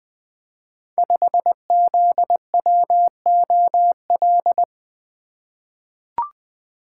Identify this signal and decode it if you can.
Morse code, heard as a keyed tone, about 21 wpm, 700 Hz. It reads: 5ZWOL
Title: Code Groups - 5 Letter:Numbers - 20wpm